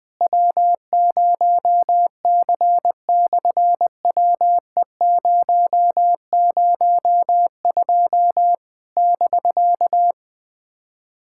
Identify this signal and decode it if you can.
Morse code, heard as a keyed tone, about 20 wpm, 700 Hz. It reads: W0C/WE002 <BK>
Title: SOTA QSO Practice 20wpm